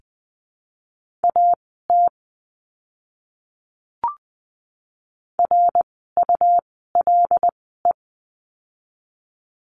Morse code, heard as a keyed tone, about 20 wpm, spaced as for 15 wpm, 700 Hz. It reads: AT RULE
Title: 101 Lesson 17f - Letter u - Words 20wpm